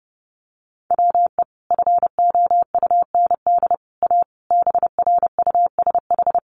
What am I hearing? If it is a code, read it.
Morse code, heard as a keyed tone, about 30 wpm, 700 Hz. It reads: WE FOUND A BRUSH
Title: Sentences from Top 1100 Words 30wpm